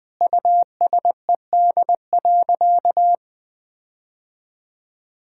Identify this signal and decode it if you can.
Morse code, heard as a keyed tone, about 21 wpm, 700 Hz. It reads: USED.
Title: Sentences from Top 1600 Words 20wpm